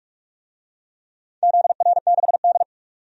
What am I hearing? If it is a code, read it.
Morse code, heard as a keyed tone, about 42 wpm, 700 Hz. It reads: ZR6D
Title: DXCC-100 Call Signs - 4 Characters - 45wpm